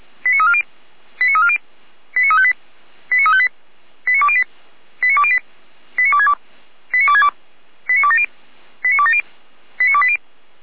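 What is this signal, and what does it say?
Five-tone selective call (ZVEI), heard as quick runs of five tones, tones of about 70 ms. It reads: 89340 89340 89358 89358 89208 89208 89262 89262 89270 89270 89280
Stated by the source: ZVEI five-tone call received on SDR, from audio server of feuerwehr-orlamuende (fire brigade Orlamünde).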